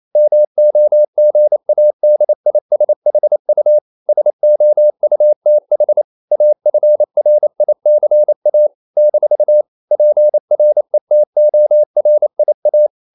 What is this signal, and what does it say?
Morse code, heard as a keyed tone, about 28 wpm, 600 Hz. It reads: MOGADISHU SOUTH AFRICA - PRETORIA
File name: CW / Practice / NationalCapitals28